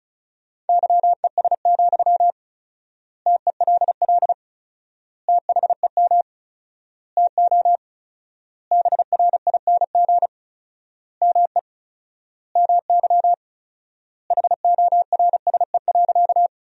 Morse code, heard as a keyed tone, about 36 wpm, 700 Hz. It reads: YES, TELL THEM TO BRING ME MY HORSE.
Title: 4x Word-Spacing - Sentences from Top 500 Words - 35wpm